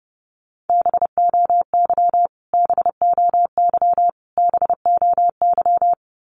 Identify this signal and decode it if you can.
Morse code, heard as a keyed tone, about 29 wpm, 700 Hz. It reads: BOY BOY BOY
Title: Top 300 Words Repeated 30wpm